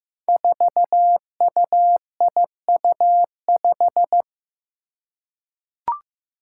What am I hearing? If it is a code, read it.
Morse code, heard as a keyed tone, about 15 wpm, 700 Hz. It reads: 4UIU5
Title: Five Challenging Letter-Number Combinations 15wpm